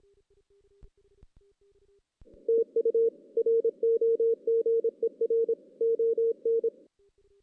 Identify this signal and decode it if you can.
Morse code, heard as a keyed tone, about 23 wpm, 410 Hz. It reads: TU ROGER ON
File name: 2016 JAN CW NAQP / 2016 Jan naqp k5ot - 0-6